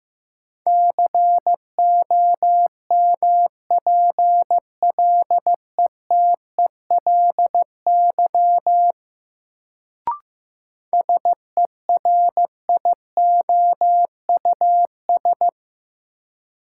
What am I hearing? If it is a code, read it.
Morse code, heard as a keyed tone, about 15 wpm, 700 Hz. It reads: COMPLETELY SERIOUS
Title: Sentences from Top 1100 Words - New Word Drill - 15wpm